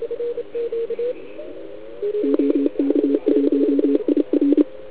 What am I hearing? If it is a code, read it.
Morse code, heard as a keyed tone, about 29 wpm, 290 Hz, with noise. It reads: OK1IR